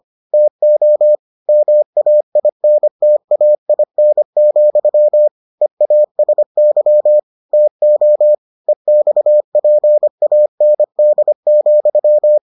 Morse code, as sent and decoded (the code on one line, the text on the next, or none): - --- / -- .- .. -. - .- .. -. --..-- / . .- ... -.-- / - --- / . -..- .--. .- -. -.. --..--
TO MAINTAIN, EASY TO EXPAND,